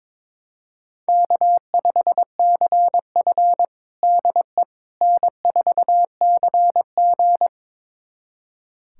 Morse code, heard as a keyed tone, about 22 wpm, 700 Hz. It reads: K5CF DE N4CG
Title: USA Callsign DE Callsign - 4 Characters - 22wpm